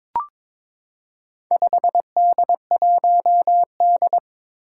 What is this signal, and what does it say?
Morse code, heard as a keyed tone, about 22 wpm, 700 Hz. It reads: 5D1D
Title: Four Challenging Letter-Number Combinations 22wpm